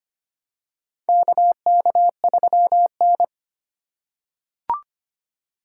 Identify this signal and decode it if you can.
Morse code, heard as a keyed tone, about 25 wpm, 700 Hz. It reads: KK3N
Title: USA Call Signs - 4 Characters - 2x Character Spacing - 25wpm